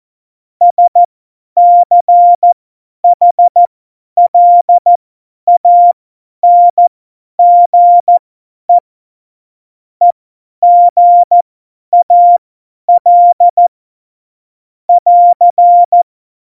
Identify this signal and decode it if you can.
Morse code, heard as a keyed tone, about 14 wpm, spaced as for 10 wpm, 700 Hz. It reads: SCHLANGE EGAL <AR>
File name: Morsekurs / Witze / Witz 22. Januar 2019, 50 F70